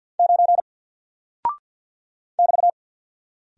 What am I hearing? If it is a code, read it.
Morse code, heard as a keyed tone, about 48 wpm, 700 Hz. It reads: 9 <BT>
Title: Single Character (Letter:Number:Punctuation:Prosign) 50wpm